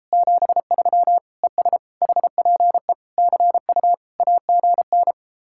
Morse code, heard as a keyed tone, about 33 wpm, 700 Hz. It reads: 73 ES HPE CU AGN